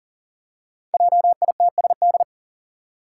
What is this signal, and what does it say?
Morse code, heard as a keyed tone, about 39 wpm, 690 Hz. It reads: JITSD
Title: Five Challenging Letter-Number Combinations 40wpm